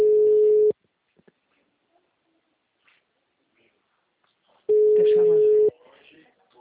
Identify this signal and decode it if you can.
Ringback tone (Europe, 425 Hz), with speech.
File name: out- -2018- - - .34069